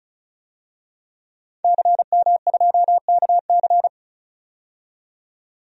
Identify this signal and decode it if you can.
Morse code, heard as a keyed tone, about 36 wpm, 700 Hz. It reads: CM2KC
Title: DXCC-100 Call Signs - 5 Characters - 35wpm